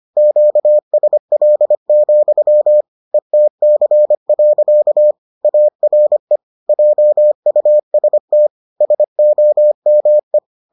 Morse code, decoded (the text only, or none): QSL, ETC. ARE JUST SOME